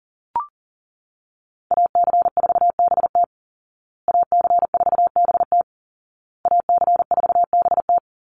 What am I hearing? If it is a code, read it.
Morse code, heard as a keyed tone, about 40 wpm, 700 Hz. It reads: AC4BT AC4BT AC4BT
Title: Call Signs Repeated 2x - 40wpm